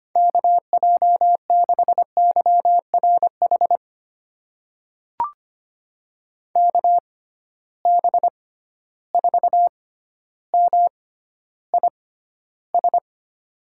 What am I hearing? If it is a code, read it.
Morse code, heard as a keyed tone, about 25 wpm, spaced as for 13 wpm, 700 Hz. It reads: KJ6YRH KB4MIS
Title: USA Call Signs - 6 Characters - 6x Character Spacing - 25wpm